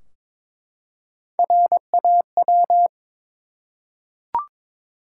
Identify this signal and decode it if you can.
Morse code, heard as a keyed tone, about 23 wpm, 700 Hz. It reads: RAW
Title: Three Letter Words - 22wpm